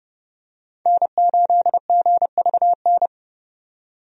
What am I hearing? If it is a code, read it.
Morse code, heard as a keyed tone, about 32 wpm, 700 Hz. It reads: N8GVN
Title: USA Call Signs - 5 Characters - 30wpm